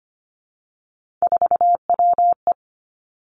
Morse code, heard as a keyed tone, about 25 wpm, 700 Hz. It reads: 4WE
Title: Three Challenging Letter-Number Combinations 25wpm